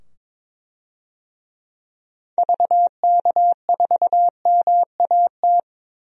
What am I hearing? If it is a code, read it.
Morse code, heard as a keyed tone, about 22 wpm, 700 Hz. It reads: VK4MAT
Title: DXCC-100 Call Signs - 6 Characters - 22wpm